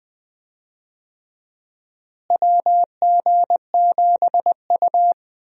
Morse code, heard as a keyed tone, about 21 wpm, 700 Hz. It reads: WG7U
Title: USA Call Signs - 4 Characters - Repeated 3x - 20wpm